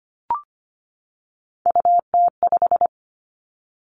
Morse code, heard as a keyed tone, about 25 wpm, 700 Hz. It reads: UT5
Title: Callsign Prefixes for Top 25 DXCC - 25wpm